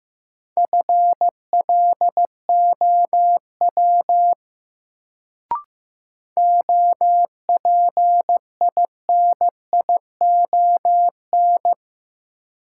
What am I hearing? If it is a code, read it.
Morse code, heard as a keyed tone, about 15 wpm, 700 Hz. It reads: FLOW OPINION
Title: Sentences from Top 1200 Words - New Word Drill - 15wpm